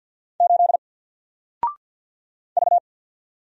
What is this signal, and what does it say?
Morse code, heard as a keyed tone, about 48 wpm, 700 Hz. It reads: 8 V
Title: Single Character (Letter:Number:Punctuation:Prosign) Alt 50wpm